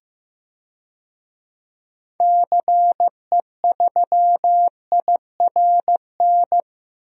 Morse code, heard as a keyed tone, about 15 wpm, 700 Hz. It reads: CE3IRN
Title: DXCC-100 Call Signs - 6 Characters - 15wpm